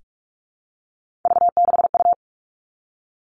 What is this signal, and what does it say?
Morse code, heard as a keyed tone, about 46 wpm, 700 Hz. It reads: V6U
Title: Three Challenging Letter-Number Combinations 45wpm